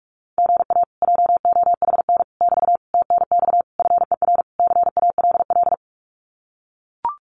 Morse code, heard as a keyed tone, about 44 wpm, 700 Hz. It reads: GA JOHN <BT> TNX FER CALL
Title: Common QSO Phrases Encore 45wpm